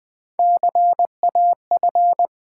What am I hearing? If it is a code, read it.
Morse code, heard as a keyed tone, about 20 wpm, 700 Hz. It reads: CAF
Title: Build A Word - 1 Letter At A Time - 4-Letter Words - 20wpm